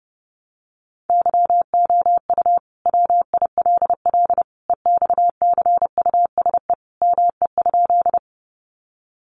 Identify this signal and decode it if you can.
Morse code, heard as a keyed tone, about 30 wpm, 700 Hz. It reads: YOU WILL EXCUSE ME?
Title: Sentences from Top 700 Words 30wpm